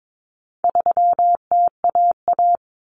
Morse code, heard as a keyed tone, about 22 wpm, 700 Hz. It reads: 3TAA
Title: Common Cut Numbers - 4-Digit 22wpm